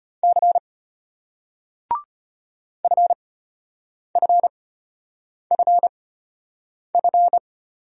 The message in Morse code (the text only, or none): C FFFF